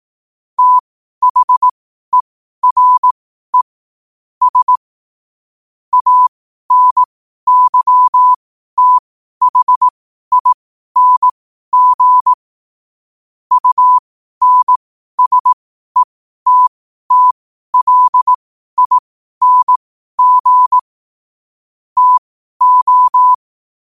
Morse code, heard as a keyed tone, about 18 wpm, spaced as for 12 wpm, 1000 Hz. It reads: THERE S ANYTHING UNSETTLING TO
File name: qotdfiles / 12wpm